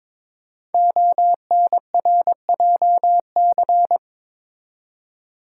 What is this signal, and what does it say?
Morse code, heard as a keyed tone, about 22 wpm, 700 Hz. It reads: ONRJC
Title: Code Groups - 5 Letters - 22wpm